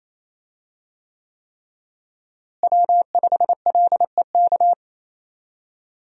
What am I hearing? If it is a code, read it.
Morse code, heard as a keyed tone, about 28 wpm, 700 Hz. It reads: W5LEK
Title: USA Call Signs - 5 Characters - Repeated 3x - 28wpm